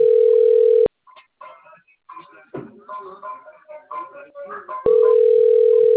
Ringback tone (North America), 440 + 480 Hz.
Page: XV de Novembro 4